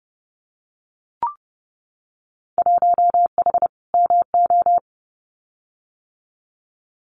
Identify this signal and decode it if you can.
Morse code, heard as a keyed tone, about 30 wpm, 700 Hz. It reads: 1H MO
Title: Winter Field Day Exchanges 30wpm